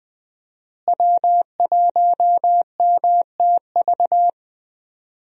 Morse code, heard as a keyed tone, about 20 wpm, 700 Hz. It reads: W1MTV
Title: USA Call Signs - 5 Characters - Repeated 3x - 20wpm